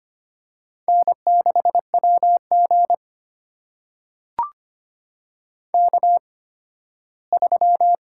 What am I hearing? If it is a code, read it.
Morse code, heard as a keyed tone, about 25 wpm, spaced as for 15 wpm, 700 Hz. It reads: N6WG K3
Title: USA Call Signs - 4 Characters - 8x Character Spacing - 25wpm